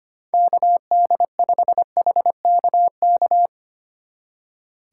Morse code, heard as a keyed tone, about 28 wpm, 700 Hz. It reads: KD5HKK
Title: Speed Racing - USA 2x3 Call Signs 25wpm